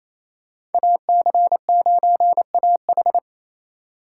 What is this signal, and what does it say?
Morse code, heard as a keyed tone, about 28 wpm, 700 Hz. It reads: AC9AH
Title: USA Call Signs - 4x Character Spacing - 28wpm